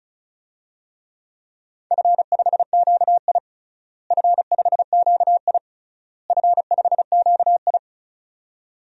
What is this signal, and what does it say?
Morse code, heard as a keyed tone, about 34 wpm, 700 Hz. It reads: F5QI F5QI F5QI
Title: DXCC-100 Call Signs - 4 Characters - Repeated 3x - 35wpm